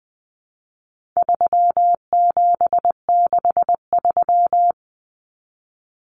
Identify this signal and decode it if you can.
Morse code, heard as a keyed tone, about 20 wpm, 700 Hz. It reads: 3763
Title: Common Cut Numbers - 4-Digit 20wpm